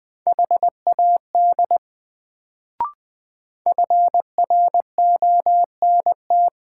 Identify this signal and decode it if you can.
Morse code, heard as a keyed tone, about 20 wpm, 700 Hz. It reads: HAD FRONT
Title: Sets of 2 Words - New Word Drill - 20wpm